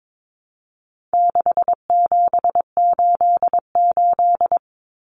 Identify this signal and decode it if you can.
Morse code, heard as a keyed tone, about 22 wpm, 700 Hz. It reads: 6788
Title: Numbers - 4-Digit 22wpm